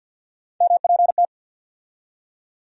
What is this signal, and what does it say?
Morse code, heard as a keyed tone, about 53 wpm, 700 Hz. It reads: MPT
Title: USA Call Sign Postfixes - 3 Letters 50wpm